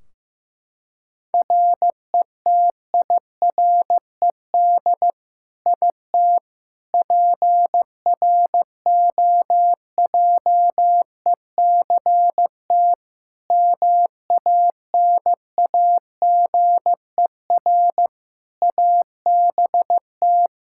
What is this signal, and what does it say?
Morse code, heard as a keyed tone, about 16 wpm, 700 Hz. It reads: RETIRED IT PROJECT MANAGER ABT
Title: Practice QSO - Exchange 3 - More Info - 15wpm